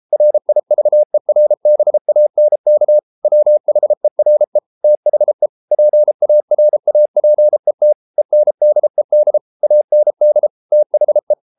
Morse code, heard as a keyed tone, about 33 wpm, 600 Hz. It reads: RIVERBANK WHERE THE PARAPET ENDED AND THE